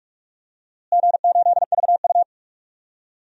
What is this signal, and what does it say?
Morse code, heard as a keyed tone, about 41 wpm, 700 Hz. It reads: G8VU
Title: DXCC-100 Call Signs - 4 Characters - Repeated 2x - 45wpm